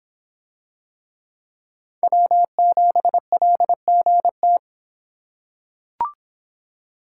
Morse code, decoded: W7LGT

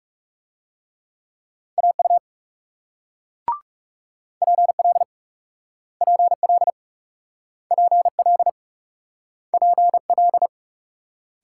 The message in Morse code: AU PL PL PL PL

